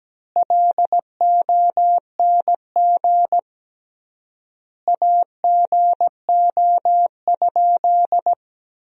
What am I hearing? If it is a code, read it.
Morse code, heard as a keyed tone, about 17 wpm, 700 Hz. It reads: LONG AGO?
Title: 3x Word-Spacing - Sentences from Top 500 Words - 17wpm